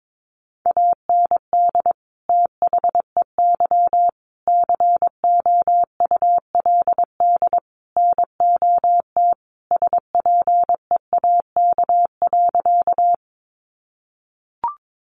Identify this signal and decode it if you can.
Morse code, heard as a keyed tone, about 22 wpm, 700 Hz. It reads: AND THEY COULD NOT SPEAK.
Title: Sentences from Top 1100 Words 22wpm